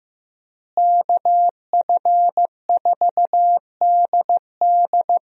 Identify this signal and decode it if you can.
Morse code, heard as a keyed tone, about 15 wpm, 700 Hz. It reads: KF4DD
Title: USA Call Signs - 5 Characters - Repeated 3x - 15wpm